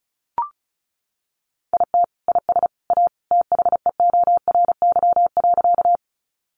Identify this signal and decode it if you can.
Morse code, heard as a keyed tone, about 34 wpm, 700 Hz. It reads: IT IS A THEORY.
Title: Sentences from Top 600 Words 35wpm